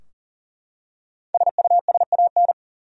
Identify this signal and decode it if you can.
Morse code, heard as a keyed tone, about 37 wpm, 700 Hz. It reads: SUSAN
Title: USA Common Names - Max 5 Letters - 40wpm